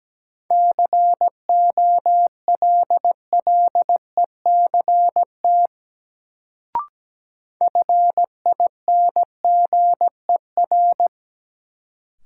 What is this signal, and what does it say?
Morse code, heard as a keyed tone, about 17 wpm, 700 Hz. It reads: COLLECT FINGER
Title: Sentences from Top 1600 Words - New Word Drill Alt - 17wpm